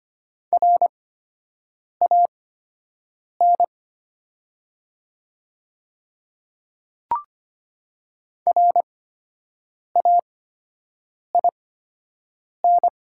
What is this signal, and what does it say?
Morse code, heard as a keyed tone, about 25 wpm, spaced as for 7 wpm, 700 Hz. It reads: RAN RAIN
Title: 8x Character Spacing - Top 500 Words - 25wpm